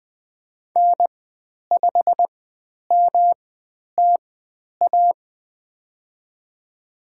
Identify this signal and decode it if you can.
Morse code, heard as a keyed tone, about 20 wpm, spaced as for 10 wpm, 700 Hz. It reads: N5MTA
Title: 109 Lesson 18g - Letter m - Call Signs 20wpm